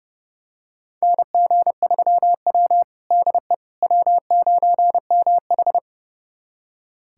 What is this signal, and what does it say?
Morse code, heard as a keyed tone, about 30 wpm, 700 Hz. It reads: NG3W DE W9MH
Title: USA Callsign DE Callsign - 4 Characters - 30wpm